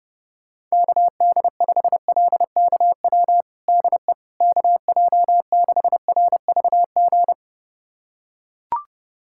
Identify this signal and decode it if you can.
Morse code, heard as a keyed tone, about 32 wpm, 700 Hz. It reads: KD5LKW DE KJ6RVG
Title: USA Callsign DE Callsign - 6 Characters - 30wpm